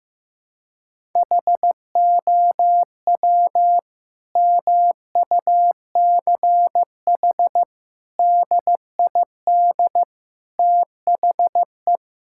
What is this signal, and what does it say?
Morse code, heard as a keyed tone, about 15 wpm, 700 Hz. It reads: HOW MUCH DID THE
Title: Sentences from Top 1000 Words - Review Part 6 - 15wpm